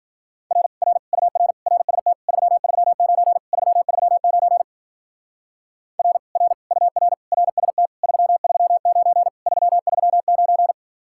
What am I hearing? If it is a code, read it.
Morse code, heard as a keyed tone, about 52 wpm, 700 Hz. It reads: R R UR RST 339 339 R R UR RST 339 339